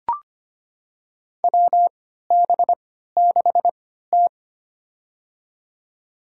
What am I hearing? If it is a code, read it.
Morse code, heard as a keyed tone, about 25 wpm, spaced as for 14 wpm, 700 Hz. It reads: WB6T
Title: USA Call Signs - 4 Characters - 3x Character Spacing - 25wpm